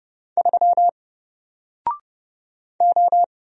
Morse code, heard as a keyed tone, about 30 wpm, 700 Hz. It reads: 3 O
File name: Single-Letter-Number-30wpm